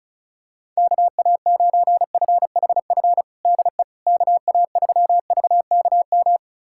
Morse code, heard as a keyed tone, about 36 wpm, 700 Hz. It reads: KA9FHF DE KA3VKM
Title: USA Callsign DE Callsign - 6 Characters - 35wpm